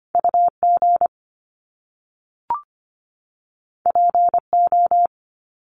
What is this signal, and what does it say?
Morse code, heard as a keyed tone, about 25 wpm, 700 Hz. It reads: UG PO
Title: Top Two Letter Combinations in Words 25wpm